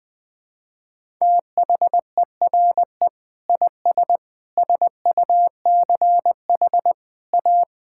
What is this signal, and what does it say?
Morse code, heard as a keyed tone, about 20 wpm, 700 Hz. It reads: THERE IS SUCH A
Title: Sentences with 6-Word Phrases - 20wpm